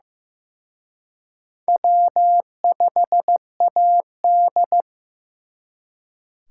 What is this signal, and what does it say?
Morse code, heard as a keyed tone, about 18 wpm, 700 Hz. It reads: W5AD
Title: Speed Racing - USA Call Signs 15wpm